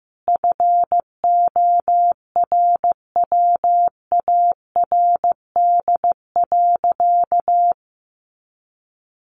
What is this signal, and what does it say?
Morse code, heard as a keyed tone, about 15 wpm, 700 Hz. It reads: FORWARD.
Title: Sentences from Top 1300 Words 15wpm